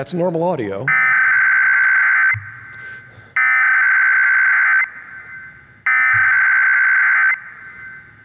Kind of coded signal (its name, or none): Emergency Alert System (SAME) tones